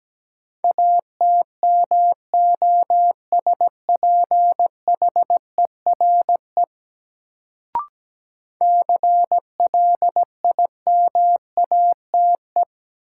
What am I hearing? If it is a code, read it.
Morse code, heard as a keyed tone, about 17 wpm, 700 Hz. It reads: ATMOSPHERE CLIMATE
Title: Sentences from Top 1600 Words - New Word Drill - 17wpm